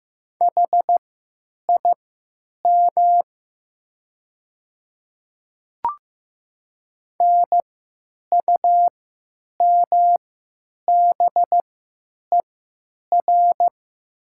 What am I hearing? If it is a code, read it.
Morse code, heard as a keyed tone, about 15 wpm, spaced as for 9 wpm, 700 Hz. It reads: HIM NUMBER
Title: 3x Character Spacing - Top 100 Words - 15wpm